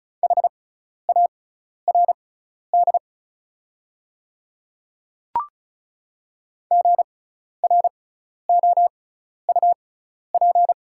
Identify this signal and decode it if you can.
Morse code, heard as a keyed tone, about 34 wpm, spaced as for 12 wpm, 690 Hz. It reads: HARD GROUP
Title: 6x Character Spacing - Top 300 Words - 35wpm